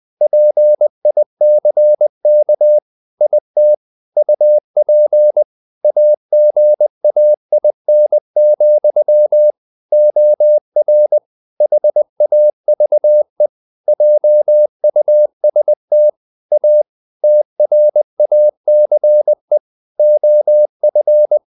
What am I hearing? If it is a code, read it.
Morse code, heard as a keyed tone, about 20 wpm, 600 Hz. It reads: PICK IT UP AGAIN, OR HAVE JUST A TRACE OF
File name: the old man and the sea part 2 20wpm